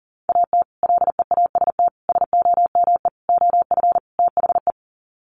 Morse code, heard as a keyed tone, about 40 wpm, 700 Hz. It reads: AT LEAST SOME OF THE
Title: Sets of 5 Words Encore 40wpm